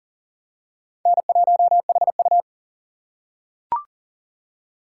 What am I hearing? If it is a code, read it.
Morse code, heard as a keyed tone, about 40 wpm, 700 Hz. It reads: N1HU
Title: USA Call Signs - 4 Characters - 4x Character Spacing - 40wpm